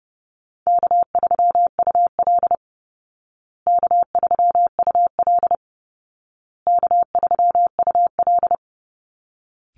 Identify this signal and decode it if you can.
Morse code, heard as a keyed tone, about 30 wpm, 700 Hz. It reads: K3UL K3UL K3UL
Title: Call Signs Repeated 2x - 30wpm